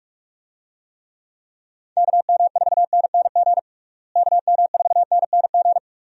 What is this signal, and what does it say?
Morse code, heard as a keyed tone, about 46 wpm, 700 Hz. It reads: KM4NNG KM4NNG
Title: USA Call Signs - 6 Characters - Repeated 2x - 45wpm